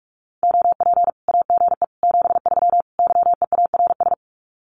Morse code, heard as a keyed tone, about 44 wpm, 700 Hz. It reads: OP AGE 73 YEARS
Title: Common QSO Phrases 45wpm